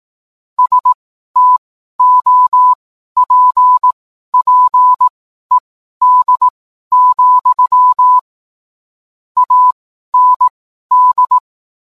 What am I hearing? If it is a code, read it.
Morse code, heard as a keyed tone, about 18 wpm, spaced as for 12 wpm, 1000 Hz. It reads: STOPPED, AND